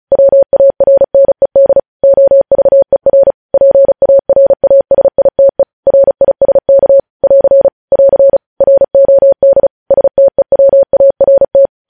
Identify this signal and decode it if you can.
Morse code, heard as a keyed tone, about 36 wpm, 550 Hz. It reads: WARNED OVER PARASITE RISK <AR> <AR> ROD STEWART
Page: news headlines in morse code at 35 wpm